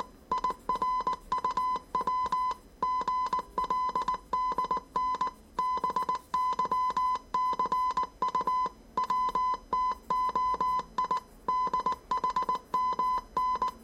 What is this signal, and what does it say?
Morse code, heard as a keyed tone, about 19 wpm, 1010 Hz. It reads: IRUW GLDN BYCU WTOI DHMN
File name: 业余无线电 / ●未分类 / 字码80